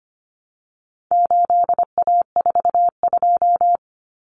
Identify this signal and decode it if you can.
Morse code, heard as a keyed tone, about 25 wpm, 700 Hz. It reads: 8A42
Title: Common Cut Numbers - 4-Digit 25wpm